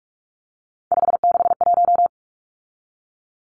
Morse code, heard as a keyed tone, about 44 wpm, 700 Hz. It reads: 561
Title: Numbers - 3-Digit 45wpm